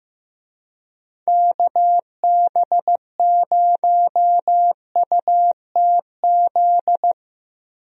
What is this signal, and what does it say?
Morse code, heard as a keyed tone, about 15 wpm, 700 Hz. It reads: KB0UTZ
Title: USA Call Signs - 6 Characters - Repeated 3x - 15wpm